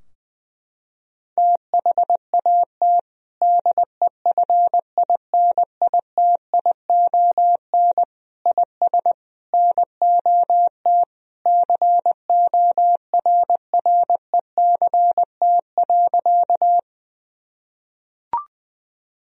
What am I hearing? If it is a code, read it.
Morse code, heard as a keyed tone, about 20 wpm, 700 Hz. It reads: THAT DEFINITION IS NOT CORRECT.
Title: Sentences from Top 1000 Words - Review Part 4 - 20wpm